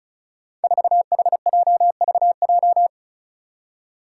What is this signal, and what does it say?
Morse code, heard as a keyed tone, about 36 wpm, 700 Hz. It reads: 4HJVJ
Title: Five Challenging Letter-Number Combinations 35wpm